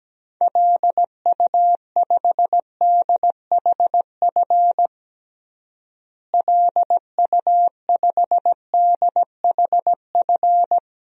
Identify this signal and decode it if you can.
Morse code, heard as a keyed tone, about 17 wpm, 700 Hz. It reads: LU5DHF LU5DHF
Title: DXCC-100 Call Signs - 6 Characters - Repeated 3x - 17wpm